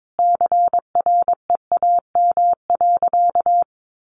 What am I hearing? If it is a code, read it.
Morse code, heard as a keyed tone, about 22 wpm, 700 Hz. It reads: CREAM.
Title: Sentences from Top 1500 Words 22wpm